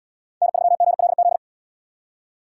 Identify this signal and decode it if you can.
Morse code, heard as a keyed tone, about 78 wpm, 700 Hz. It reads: N3GLZ